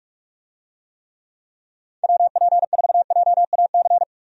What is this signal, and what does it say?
Morse code, heard as a keyed tone, about 41 wpm, 700 Hz. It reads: WP4JAC
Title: USA Call Signs - 6 Characters - 45wpm